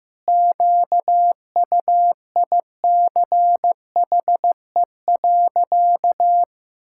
Morse code, heard as a keyed tone, about 15 wpm, 700 Hz. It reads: QUICHE.
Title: Pangram Challenge v2 - 15wpm